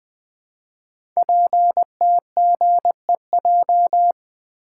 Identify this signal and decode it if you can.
Morse code, heard as a keyed tone, about 20 wpm, 700 Hz. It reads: PTGEJ
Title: Code Groups - 5 Letters - 20wpm